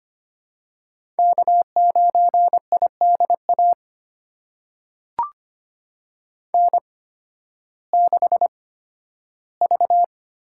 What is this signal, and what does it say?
Morse code, heard as a keyed tone, about 25 wpm, spaced as for 13 wpm, 700 Hz. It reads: K9IDA N6V
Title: USA Call Signs - 5 Characters - 8x Character Spacing - 25wpm